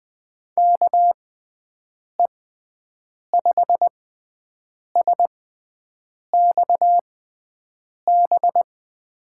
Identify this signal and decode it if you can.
Morse code, heard as a keyed tone, about 20 wpm, spaced as for 7 wpm, 700 Hz. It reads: KE5SXB
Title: USA Call Signs - 6 Characters - 6x Character Spacing - 20wpm